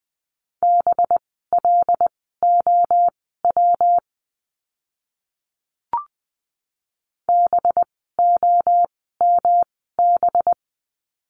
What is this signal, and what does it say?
Morse code, heard as a keyed tone, about 20 wpm, spaced as for 14 wpm, 700 Hz. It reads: BLOW BOMB
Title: 185 Lesson 30f - Letter b - Words 20wpm